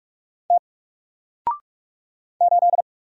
Morse code, heard as a keyed tone, about 44 wpm, 700 Hz. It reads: T 8
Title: Single Character (Letter:Number:Punctuation:Prosign) Alt 45wpm